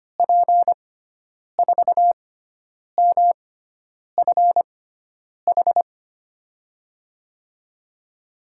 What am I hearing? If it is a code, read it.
Morse code, heard as a keyed tone, about 25 wpm, spaced as for 9 wpm, 700 Hz. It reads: P4MFH
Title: USA Call Signs - 6 Characters - 6x Character Spacing - 25wpm